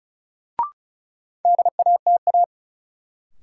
Morse code, heard as a keyed tone, about 36 wpm, 700 Hz. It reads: DATA
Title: Sentences from Top 1000 Words - Review Part 3 - New Word Drill - 35wpm